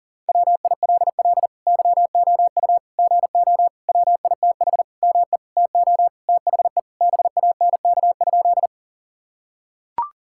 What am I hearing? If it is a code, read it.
Morse code, heard as a keyed tone, about 40 wpm, 700 Hz. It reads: WILL YOU GO WITH ME TO THE BANK?
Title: Sentences from Top 1000 Words - Review Part 6 - 40wpm